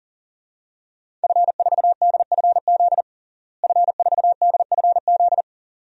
Morse code, heard as a keyed tone, about 40 wpm, 700 Hz. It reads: F4DFZ F4DFZ
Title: DXCC-100 Call Signs - 5 Characters - Repeated 2x - 40wpm